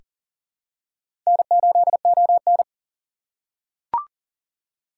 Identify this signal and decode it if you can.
Morse code, heard as a keyed tone, about 37 wpm, 700 Hz. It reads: N8ON
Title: USA Call Signs - 4 Characters - 2x Character Spacing - 40wpm